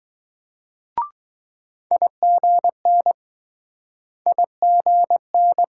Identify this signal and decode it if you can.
Morse code, heard as a keyed tone, about 20 wpm, 700 Hz. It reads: IGN IGN